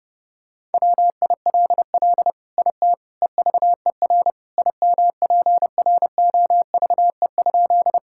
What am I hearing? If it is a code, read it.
Morse code, heard as a keyed tone, about 30 wpm, 700 Hz. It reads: WILL IT EVER IMPROVE?
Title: Sentences from Top 1000 Words - Review Part 5 - 30wpm